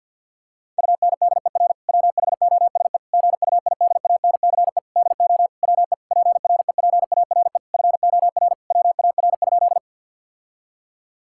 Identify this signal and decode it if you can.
Morse code, heard as a keyed tone, about 51 wpm, 700 Hz. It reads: UNDER WHOSE GUIDANCE DO WE PREPARE FOR WAR?
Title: Sentences from Top 1600 Words 50wpm